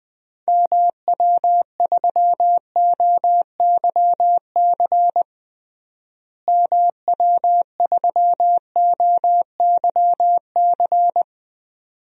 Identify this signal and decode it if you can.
Morse code, heard as a keyed tone, about 20 wpm, 700 Hz. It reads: MW3OYC MW3OYC
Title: DXCC-100 Call Signs - 6 Characters - Repeated 2x - 20wpm